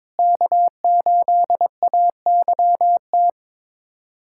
Morse code, heard as a keyed tone, about 21 wpm, 700 Hz. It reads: K8AYT
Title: USA Callsign DE Callsign 22wpm